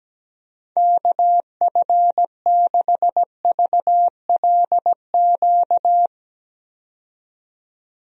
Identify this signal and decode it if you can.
Morse code, heard as a keyed tone, about 17 wpm, 700 Hz. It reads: KF6VLQ
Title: USA Call Signs - 6 Characters - 17wpm